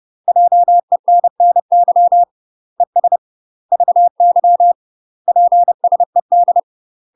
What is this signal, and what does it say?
Morse code, heard as a keyed tone, about 30 wpm, 700 Hz. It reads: JENNY ES VY PSED
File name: UK MASTER QSOs TRAINING / 30WPM / UK MASTER QSOs 0050